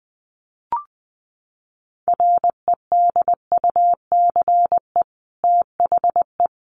Morse code, heard as a keyed tone, about 20 wpm, 700 Hz. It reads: REDUCE THE
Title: Sentences from Top 1100 Words 20wpm